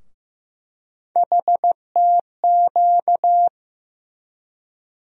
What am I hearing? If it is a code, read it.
Morse code, heard as a keyed tone, about 15 wpm, 700 Hz. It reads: HTQ
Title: USA Call Sign Postfixes - 3 Letters 15wpm